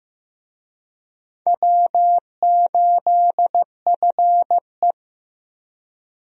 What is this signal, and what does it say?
Morse code, heard as a keyed tone, about 15 wpm, 700 Hz. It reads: W8FE
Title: USA Call Signs - 4 Characters - 15wpm